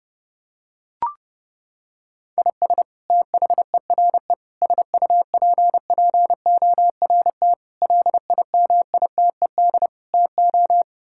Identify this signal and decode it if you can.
Morse code, heard as a keyed tone, about 30 wpm, 700 Hz. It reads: IS THERE SUPPORT LIMITED TO
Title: Sentences from Top 1000 Words - Review Part 6 - 30wpm